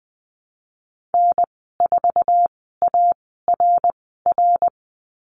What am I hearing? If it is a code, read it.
Morse code, heard as a keyed tone, about 20 wpm, spaced as for 14 wpm, 700 Hz. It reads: N4ARR
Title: 062 Lesson 11g - Letter h - Call Signs 20wpm